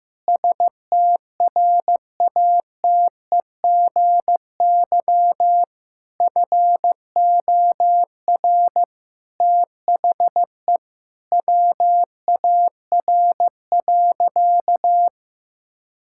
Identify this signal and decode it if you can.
Morse code, heard as a keyed tone, about 15 wpm, 700 Hz. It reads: STRATEGY FOR THE WAR.
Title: Sentences from Top 1000 Words - Review Part 5 - 15wpm